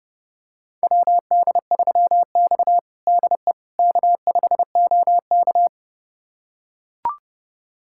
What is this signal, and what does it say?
Morse code, heard as a keyed tone, about 30 wpm, 700 Hz. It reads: WD3X DE K5OK
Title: USA Callsign DE Callsign - 4 Characters - 30wpm